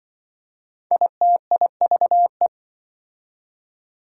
Morse code, heard as a keyed tone, about 25 wpm, 700 Hz. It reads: ITIVE
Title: Speed-Racing Prefixes and Suffixes 20wpm